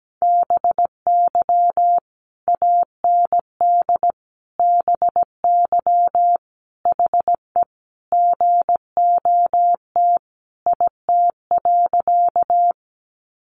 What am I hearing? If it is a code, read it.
Morse code, heard as a keyed tone, about 17 wpm, 700 Hz. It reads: BY AND BY HE GOT IT.
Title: Sentences from Top 300 Words 17wpm